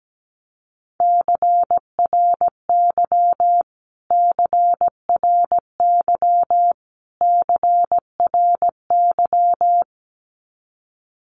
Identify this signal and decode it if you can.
Morse code, heard as a keyed tone, about 17 wpm, 700 Hz. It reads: CRY CRY CRY
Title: Top 500 Words Repeated 17wpm